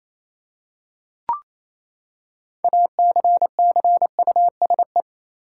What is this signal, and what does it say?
Morse code, heard as a keyed tone, about 28 wpm, 700 Hz. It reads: ACCUSE
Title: Six Letter Words - 28wpm